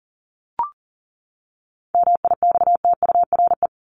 Morse code, heard as a keyed tone, about 40 wpm, 700 Hz. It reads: MIXTURE
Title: Top 1251-1500 Words 40wpm